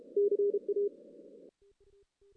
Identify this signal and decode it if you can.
Morse code, heard as a keyed tone, about 30 wpm, 410 Hz. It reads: CA
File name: 2016 JAN CW NAQP / 2016 Jan naqp k5ot - 0-6